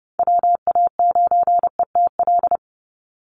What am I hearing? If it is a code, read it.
Morse code, heard as a keyed tone, about 30 wpm, 700 Hz. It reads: WA9ETL
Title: USA Call Signs 30wpm